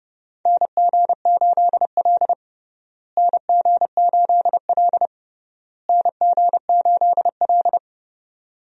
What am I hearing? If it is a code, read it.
Morse code, heard as a keyed tone, about 30 wpm, 700 Hz. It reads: NG8L NG8L NG8L
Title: USA Call Signs - 4 Characters - Repeated 3x - 30wpm